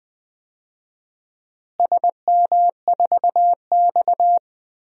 Morse code, heard as a keyed tone, about 20 wpm, 700 Hz. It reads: SM4X
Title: DXCC-100 Call Signs - 4 Characters - Repeated 3x - 20wpm